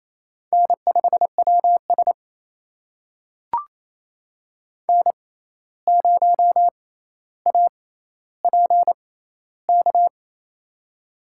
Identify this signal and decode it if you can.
Morse code, heard as a keyed tone, about 28 wpm, spaced as for 13 wpm, 700 Hz. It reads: N5WS N0APK
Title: USA Call Signs - 6x Character Spacing - 28wpm